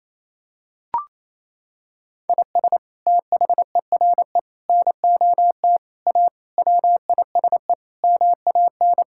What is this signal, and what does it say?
Morse code, heard as a keyed tone, about 28 wpm, 700 Hz. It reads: IS THERE NOT A WISE MAN
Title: Sentences from Top 1600 Words 28wpm